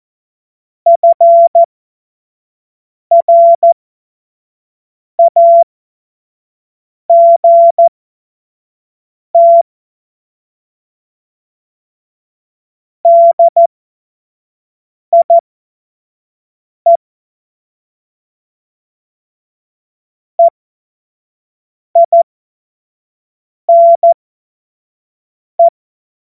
Morse code, heard as a keyed tone, about 14 wpm, spaced as for 5 wpm, 680 Hz. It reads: FRAGT DIE EINE
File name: Morsekurs / Witze / Witz 13. Dezember 2022 25 F70